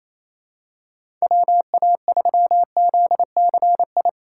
Morse code, heard as a keyed tone, about 28 wpm, 700 Hz. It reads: WA3ZCI
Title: USA Call Signs - 4x Character Spacing - 28wpm